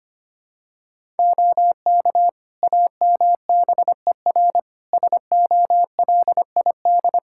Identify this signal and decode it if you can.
Morse code, heard as a keyed tone, about 25 wpm, 700 Hz. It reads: OK AMBER SOLID
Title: Practice QSO - Exchange 3 - More Info - 25wpm